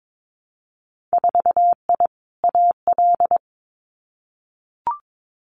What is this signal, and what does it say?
Morse code, heard as a keyed tone, about 22 wpm, 700 Hz. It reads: 4I AL
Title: Winter Field Day Exchanges Encore 22wpm